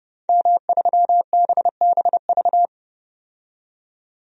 Morse code, heard as a keyed tone, about 30 wpm, 700 Hz. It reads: M3BBV
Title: DXCC-100 Call Signs - 5 Characters - 30wpm